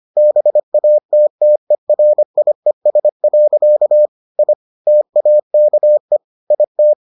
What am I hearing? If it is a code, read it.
Morse code, heard as a keyed tone, about 25 wpm, 600 Hz. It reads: BATTERIES. I TAKE IT